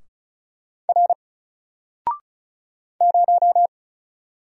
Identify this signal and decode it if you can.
Morse code, heard as a keyed tone, about 35 wpm, 700 Hz. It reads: R 0
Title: Single Letter-Number 35wpm